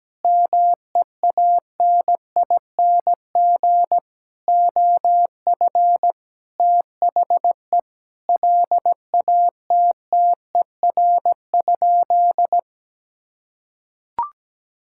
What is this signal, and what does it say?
Morse code, heard as a keyed tone, about 17 wpm, 700 Hz. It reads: MEANING OF THE LATTER?
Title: Sentences from Top 1600 Words 17wpm